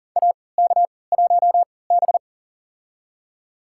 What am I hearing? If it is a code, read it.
Morse code, heard as a keyed tone, about 40 wpm, spaced as for 23 wpm, 700 Hz. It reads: AK1B
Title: USA Call Signs - 4 Characters - 3x Character Spacing - 40wpm